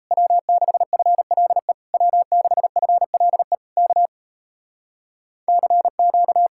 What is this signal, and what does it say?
Morse code, heard as a keyed tone, about 37 wpm, 700 Hz. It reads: W6FLE W6FLE K CQ